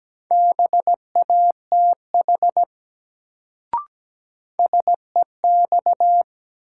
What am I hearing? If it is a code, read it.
Morse code, heard as a keyed tone, about 17 wpm, 700 Hz. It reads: BATH SEX
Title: Sentences from Top 1600 Words - New Word Drill - 17wpm